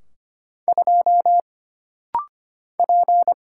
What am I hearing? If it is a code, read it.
Morse code, heard as a keyed tone, about 25 wpm, 700 Hz. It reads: 2 P
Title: Single Letter-Number Alt - Rapid-Fire 25wpm